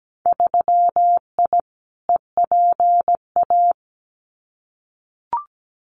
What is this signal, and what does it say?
Morse code, heard as a keyed tone, about 17 wpm, 700 Hz. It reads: 3I EPA
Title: Winter Field Day Exchanges 17wpm